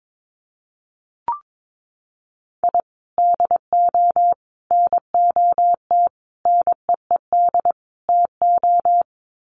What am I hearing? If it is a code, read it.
Morse code, heard as a keyed tone, about 22 wpm, 700 Hz. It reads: I DO NOT NEED TO
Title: Sets of 5 Words 22wpm